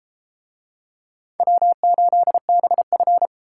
Morse code, heard as a keyed tone, about 32 wpm, 700 Hz. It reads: W8BF